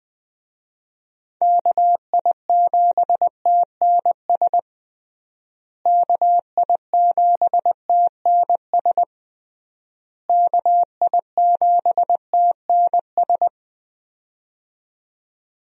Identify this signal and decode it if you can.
Morse code, heard as a keyed tone, about 20 wpm, 700 Hz. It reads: KI7TNS KI7TNS KI7TNS
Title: USA Call Signs - 6 Characters - Repeated 3x - 20wpm